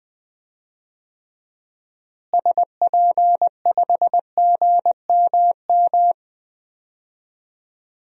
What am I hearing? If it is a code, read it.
Morse code, heard as a keyed tone, about 20 wpm, 700 Hz. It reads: SP5GMM
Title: DXCC-100 Call Signs - 6 Characters - Repeated 2x - 20wpm